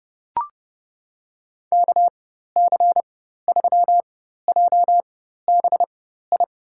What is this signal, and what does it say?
Morse code, heard as a keyed tone, about 30 wpm, spaced as for 14 wpm, 700 Hz. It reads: KC3JBI
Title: USA Call Signs - 6 Characters - 4x Character Spacing - 30wpm